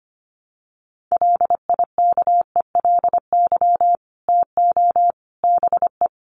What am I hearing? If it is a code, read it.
Morse code, heard as a keyed tone, about 25 wpm, 700 Hz. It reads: LIKELY TO BE
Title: Sets of 3 Words 25wpm